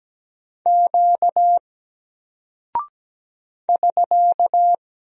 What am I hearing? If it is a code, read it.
Morse code, heard as a keyed tone, about 17 wpm, 700 Hz. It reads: Q <SK>
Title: Single Character (Letter:Number:Punctuation:Prosign) Alt 17wpm